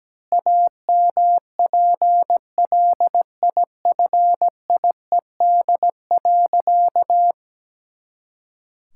Morse code, heard as a keyed tone, about 17 wpm, 700 Hz. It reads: AMPLIFIED.
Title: Pangram Challenge v2 - 17wpm